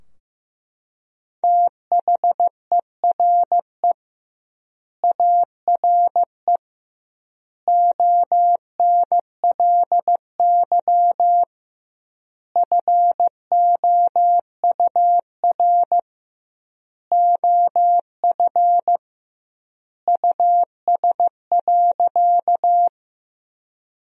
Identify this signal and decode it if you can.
Morse code, heard as a keyed tone, about 15 wpm, 700 Hz. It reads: THERE ARE ONLY FOUR OF US.
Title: 2x Word-Spacing - Sentences from Top 300 Words - 15wpm